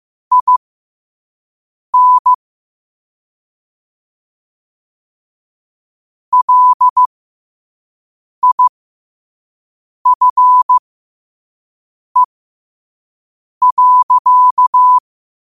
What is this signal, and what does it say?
Morse code, heard as a keyed tone, about 15 wpm, spaced as for 5 wpm, 1000 Hz. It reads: IN LIFE.
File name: qotdfiles / 5wpm